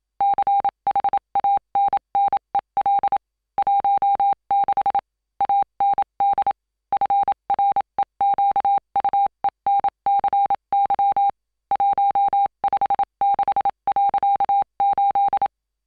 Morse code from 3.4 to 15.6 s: 16 AND FREQUENCY 156.8